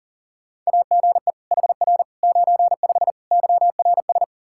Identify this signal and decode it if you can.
Morse code, heard as a keyed tone, about 40 wpm, 700 Hz. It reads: AGE HR 95 YRS
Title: Practice QSO - Exchange 3 - More Info - 40wpm